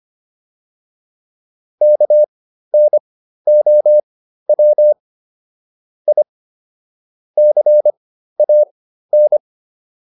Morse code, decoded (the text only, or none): KNOW I CAN